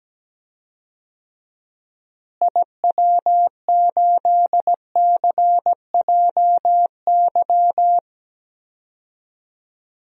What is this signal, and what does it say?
Morse code, heard as a keyed tone, about 17 wpm, 700 Hz. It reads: IW8CJY
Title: DXCC-100 Call Signs - 6 Characters - Repeated 2x - 17wpm